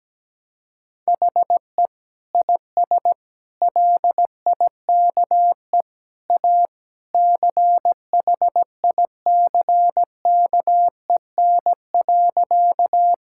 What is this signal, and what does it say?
Morse code, heard as a keyed tone, about 17 wpm, 700 Hz. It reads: HE IS LIKE A CHICKEN.
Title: Sentences from Top 1000 Words - Review Part 5 - 17wpm